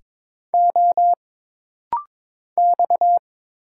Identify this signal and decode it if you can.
Morse code, heard as a keyed tone, about 22 wpm, 700 Hz. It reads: O X
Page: This rapid-fire timing is geared to push your ICR (Instant Character Recognition) skills.